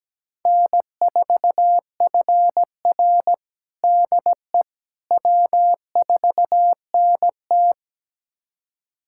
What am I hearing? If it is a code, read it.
Morse code, heard as a keyed tone, about 17 wpm, 700 Hz. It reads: N4FR DE W4NT
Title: USA Callsign DE Callsign - 4 Characters - 17wpm